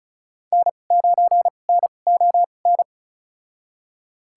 Morse code, heard as a keyed tone, about 34 wpm, spaced as for 25 wpm, 690 Hz. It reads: N9NON
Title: USA Call Signs - 5 Characters - 2x Character Spacing - 35wpm